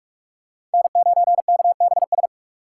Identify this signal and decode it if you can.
Morse code, heard as a keyed tone, about 46 wpm, 700 Hz. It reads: N9KBS
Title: USA Call Signs - 5 Characters - 6x Character Spacing - 45wpm